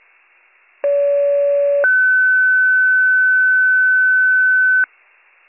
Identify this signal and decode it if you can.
Two-tone page, heard as a one-second tone, then a three-second tone, about 570 then 1500 Hz. It is heard from 0.8 to 4.9 s.